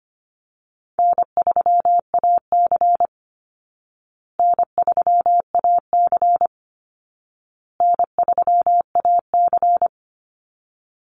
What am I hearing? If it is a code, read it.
Morse code, heard as a keyed tone, about 25 wpm, 700 Hz. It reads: N3AC N3AC N3AC
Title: Easy Call Signs Repeated 3x - 25wpm